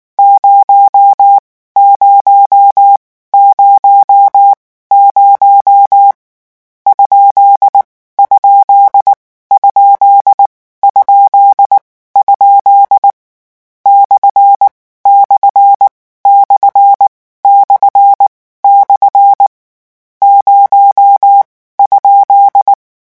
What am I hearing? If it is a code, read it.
Morse code, heard as a keyed tone, about 19 wpm, spaced as for 14 wpm, 800 Hz. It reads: 0 0 00 ? ? ??? /// // 0?
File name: morse mp3 / lecon13-vitesse 18